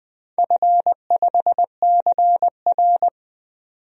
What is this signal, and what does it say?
Morse code, heard as a keyed tone, about 20 wpm, 700 Hz. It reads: F5CR
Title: DXCC-100 Call Signs - 4 Characters - 20wpm